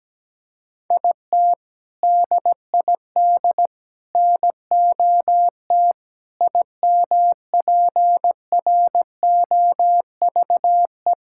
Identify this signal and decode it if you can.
Morse code, heard as a keyed tone, about 17 wpm, 700 Hz. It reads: IT DID NOT IMPROVE
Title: Sentences from Top 1000 Words - Review Part 5 - 17wpm